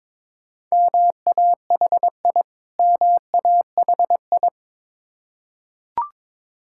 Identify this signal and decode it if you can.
Morse code, heard as a keyed tone, about 22 wpm, 690 Hz. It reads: MAHI MAHI
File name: Difficult Words with Hints - 22wpm